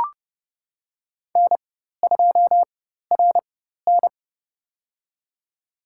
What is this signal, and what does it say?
Morse code, heard as a keyed tone, about 30 wpm, spaced as for 14 wpm, 700 Hz. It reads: N2RN
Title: USA Call Signs - 4 Characters - 4x Character Spacing - 30wpm